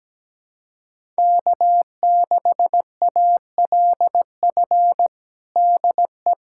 Morse code, heard as a keyed tone, about 17 wpm, 700 Hz. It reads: K6ALF DE
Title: USA Callsign DE Callsign - 5 Characters - 17wpm